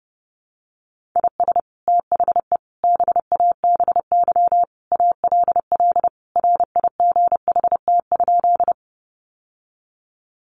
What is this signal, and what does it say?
Morse code, heard as a keyed tone, about 30 wpm, 700 Hz. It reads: IS THE BABY ALL RIGHT?
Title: Sentences from Top 1300 Words 30wpm